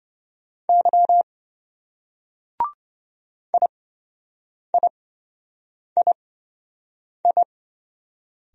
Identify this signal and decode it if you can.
Morse code, heard as a keyed tone, about 26 wpm, spaced as for 7 wpm, 700 Hz. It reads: Y IIII